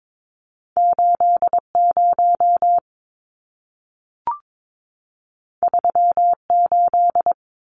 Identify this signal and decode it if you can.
Morse code, heard as a keyed tone, about 22 wpm, 700 Hz. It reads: 80 38
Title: Numbers - 2-Digit 22wpm